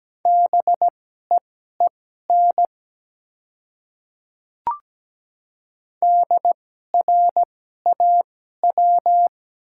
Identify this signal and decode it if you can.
Morse code, heard as a keyed tone, about 17 wpm, spaced as for 12 wpm, 700 Hz. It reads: BEEN DRAW
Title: 2x Character Spacing - Top 300 Words - 17wpm